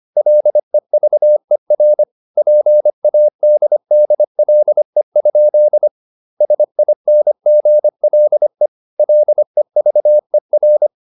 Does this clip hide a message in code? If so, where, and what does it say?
Yes: Morse code from 0.2 to 10.9 s, heard as a keyed tone, about 25 wpm, 600 Hz. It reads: LEVER PADDLE? SINGLE LEVER